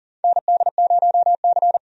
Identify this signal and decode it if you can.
Morse code, heard as a keyed tone, about 40 wpm, 700 Hz. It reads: ND0C
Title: USA Call Signs - 4 Characters - 40wpm